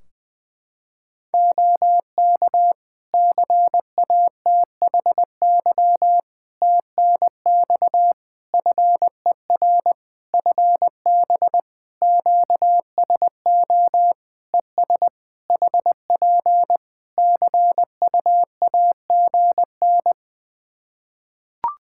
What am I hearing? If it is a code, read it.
Morse code, heard as a keyed tone, about 20 wpm, 700 Hz. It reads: OK CATHY TNX FER FB QSO ES HP CUAGN
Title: K1USN SST QSO Practice 20wpm